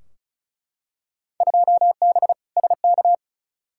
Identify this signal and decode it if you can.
Morse code, heard as a keyed tone, about 36 wpm, 700 Hz. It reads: 2B SK
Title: Field Day Exchanges Encore - 35wpm